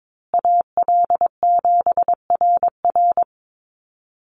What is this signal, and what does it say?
Morse code, heard as a keyed tone, about 22 wpm, 700 Hz. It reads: AL7RR
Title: USA Call Signs Repeated 3x 22wpm